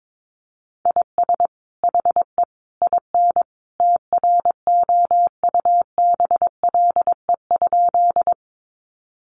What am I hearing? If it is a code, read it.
Morse code, heard as a keyed tone, about 22 wpm, 700 Hz. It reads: IS HE IN TROUBLE?
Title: Sentences from Top 1200 Words 22wpm